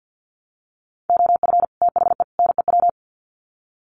Morse code, heard as a keyed tone, about 50 wpm, 700 Hz. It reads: OF THE NEW
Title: Sets of 3 Words 50wpm